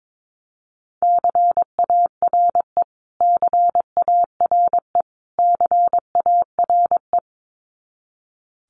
Morse code, heard as a keyed tone, about 22 wpm, 700 Hz. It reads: CARE CARE CARE
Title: Top 500 Words Repeated 22wpm